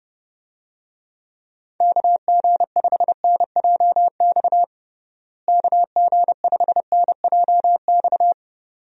Morse code, heard as a keyed tone, about 32 wpm, 700 Hz. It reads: KG5NJX KG5NJX
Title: USA Call Signs - 6 Characters - Repeated 2x - 30wpm